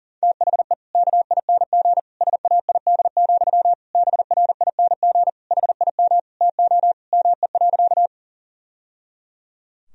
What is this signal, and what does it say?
Morse code, heard as a keyed tone, about 40 wpm, 700 Hz. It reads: THE KING SAID, BRING HIM TO ME.
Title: Sentences from Top 1000 Words - Review Part 3 - 40wpm